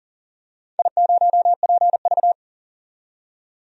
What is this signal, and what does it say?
Morse code, heard as a keyed tone, about 38 wpm, 700 Hz. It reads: I0PV
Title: DXCC-100 Call Signs - 4 Characters - 40wpm